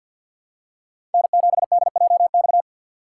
Morse code, heard as a keyed tone, about 52 wpm, 700 Hz. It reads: N7DJX
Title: USA Call Signs - 5 Characters - 2x Character Spacing - 50wpm